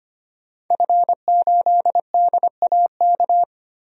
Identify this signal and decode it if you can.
Morse code, heard as a keyed tone, about 25 wpm, 700 Hz. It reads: F8DAK
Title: DXCC-100 Call Signs - 5 Characters - Repeated 3x - 25wpm